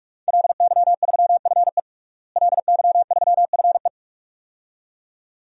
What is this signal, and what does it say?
Morse code, heard as a keyed tone, about 44 wpm, 700 Hz. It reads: LY3FE LY3FE
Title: DXCC-100 Call Signs - 5 Characters - Repeated 2x - 45wpm